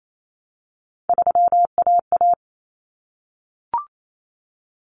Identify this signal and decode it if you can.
Morse code, heard as a keyed tone, about 28 wpm, 700 Hz. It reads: 3AA
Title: Common Cut Numbers - 3-Digit 28wpm